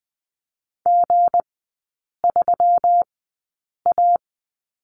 Morse code, heard as a keyed tone, about 20 wpm, spaced as for 8 wpm, 690 Hz. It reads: G3A
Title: 159 Lesson 26e - Letter g - Three Characters 20wpm